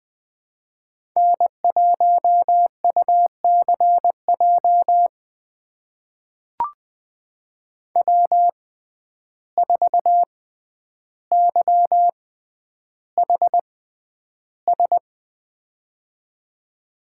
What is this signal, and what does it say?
Morse code, heard as a keyed tone, about 20 wpm, spaced as for 10 wpm, 700 Hz. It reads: N1UCJ W4YHS
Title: USA Call Signs - 5 Characters - 6x Character Spacing - 20wpm